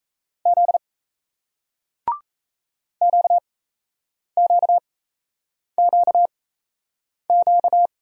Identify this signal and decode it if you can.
Morse code, heard as a keyed tone, about 38 wpm, spaced as for 8 wpm, 700 Hz. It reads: Z QQQQ